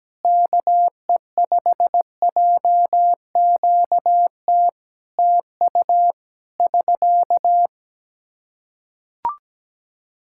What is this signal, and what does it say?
Morse code, heard as a keyed tone, about 17 wpm, 700 Hz. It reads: KE5JQT TU <SK>
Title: Practice QSO - Exchange 4 - Ending - 17wpm